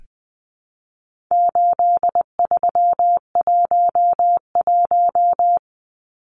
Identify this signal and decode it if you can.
Morse code, heard as a keyed tone, about 19 wpm, 700 Hz. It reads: 8311
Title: Numbers - 4-Digit 20wpm